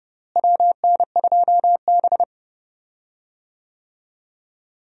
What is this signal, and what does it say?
Morse code, heard as a keyed tone, about 29 wpm, 700 Hz. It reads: WN2B
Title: USA Call Signs - 4 Characters - Repeated 3x - 30wpm